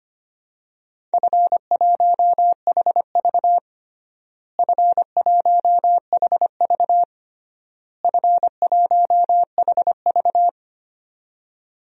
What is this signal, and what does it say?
Morse code, heard as a keyed tone, about 25 wpm, 700 Hz. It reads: F1HV F1HV F1HV
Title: DXCC-100 Call Signs - 4 Characters - Repeated 3x - 25wpm